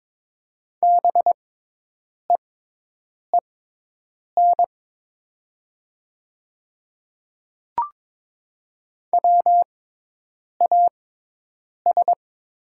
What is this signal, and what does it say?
Morse code, heard as a keyed tone, about 22 wpm, spaced as for 8 wpm, 700 Hz. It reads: BEEN WAS
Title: 6x Character Spacing - Top 200 Words - 22wpm